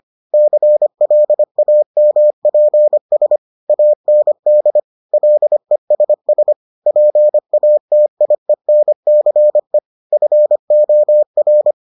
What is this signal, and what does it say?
Morse code, heard as a keyed tone, about 25 wpm, 600 Hz. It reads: CLAMPS AND LESS PATIENCE FOR